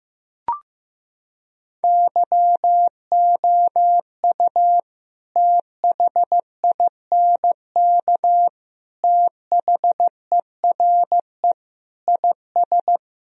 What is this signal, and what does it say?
Morse code, heard as a keyed tone, about 15 wpm, 700 Hz. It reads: YOU THINK THERE IS
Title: Sentences from Top 1000 Words - Review Part 4 - 15wpm